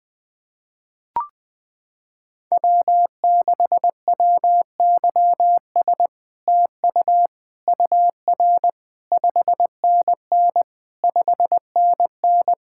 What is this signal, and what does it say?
Morse code, heard as a keyed tone, about 20 wpm, 700 Hz. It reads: W6WYS TU UR 5NN 5NN
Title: SOTA QSO Practice 20wpm